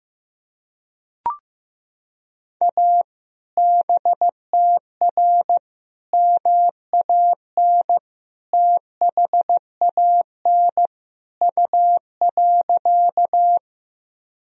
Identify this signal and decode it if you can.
Morse code, heard as a keyed tone, about 16 wpm, 700 Hz. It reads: A BTR MAN THAN U.
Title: Sentences from Top 400 Words and Abbreviations 15wpm